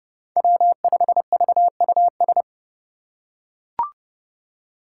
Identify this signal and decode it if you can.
Morse code, heard as a keyed tone, about 29 wpm, 700 Hz. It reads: W5VUS
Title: Five Challenging Letter-Number Combinations 30wpm